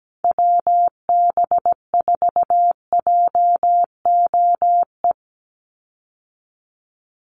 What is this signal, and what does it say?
Morse code, heard as a keyed tone, about 17 wpm, 700 Hz. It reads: WB4JOE
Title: USA Call Signs 17wpm